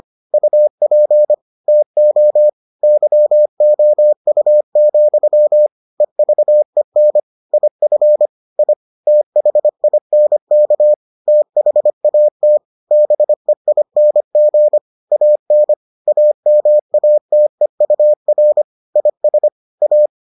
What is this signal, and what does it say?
Morse code, heard as a keyed tone, about 25 wpm, 600 Hz. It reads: UP TO YOU, EVEN IF I THINK THAT BEING AN AMATEUR IS A